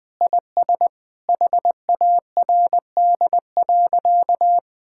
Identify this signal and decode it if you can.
Morse code, heard as a keyed tone, about 20 wpm, 700 Hz. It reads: IS HARD.
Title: Sentences with 3-Word Phrases - 20wpm